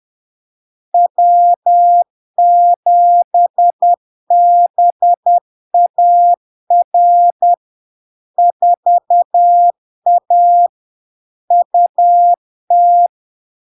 Morse code, heard as a keyed tone, about 10 wpm, 700 Hz. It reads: W7BAR 4A UT
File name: LARK-FD-2022-Log-in-CW-10wpm-700-Hz